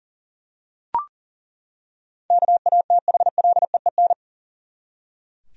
Morse code, heard as a keyed tone, about 40 wpm, 700 Hz. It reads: KATHLEEN
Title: USA Common Names - 40wpm